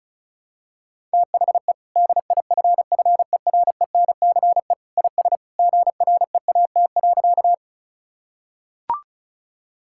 Morse code, heard as a keyed tone, about 34 wpm, 700 Hz. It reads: THE DIFFERENCE IS GREAT.
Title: Sentences from Top 1000 Words - Review Part 5 - 35wpm